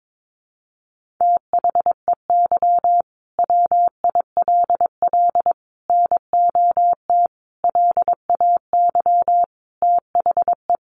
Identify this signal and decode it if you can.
Morse code, heard as a keyed tone, about 22 wpm, 700 Hz. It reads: THEY WILL NOT LAY THE
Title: Sentences from Top 1500 Words 22wpm